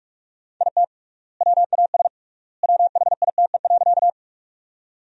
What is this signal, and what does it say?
Morse code, heard as a keyed tone, about 43 wpm, 700 Hz. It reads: IT WAS WHITE.
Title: 3x Word-Spacing - Sentences from Top 300 Words - 45wpm